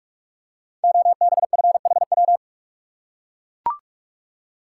Morse code, heard as a keyed tone, about 44 wpm, 700 Hz. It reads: OBFHW
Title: Code Groups - 5 Letters - 45wpm